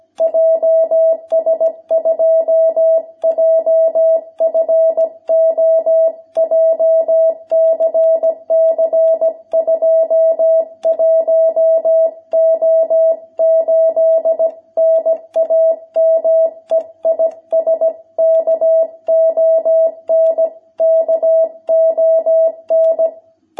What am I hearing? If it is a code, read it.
Morse code, heard as a keyed tone, about 17 wpm, 660 Hz. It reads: JS2JFO JCC 21O8 NAME IS KON KON
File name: jh8cbh / 2021cw